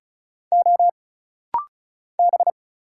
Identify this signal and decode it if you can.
Morse code, heard as a keyed tone, about 35 wpm, 700 Hz. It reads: O B
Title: Single Letter-Number - Rapid-Fire 35wpm